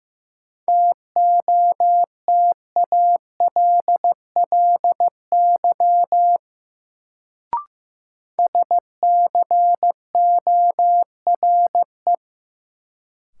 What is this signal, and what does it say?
Morse code, heard as a keyed tone, about 15 wpm, 700 Hz. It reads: TOTALLY SCORE
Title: Sentences from Top 1600 Words - New Word Drill - 15wpm